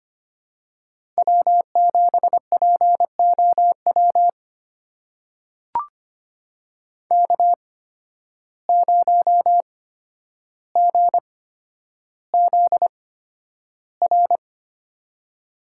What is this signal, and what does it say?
Morse code, heard as a keyed tone, about 25 wpm, spaced as for 11 wpm, 700 Hz. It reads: W7POW K0GZR
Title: USA Call Signs - 5 Characters - 8x Character Spacing - 25wpm